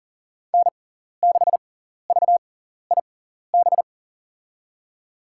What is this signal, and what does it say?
Morse code, heard as a keyed tone, about 40 wpm, spaced as for 14 wpm, 700 Hz. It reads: N6VIB
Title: USA Call Signs - 6x Character Spacing - 40wpm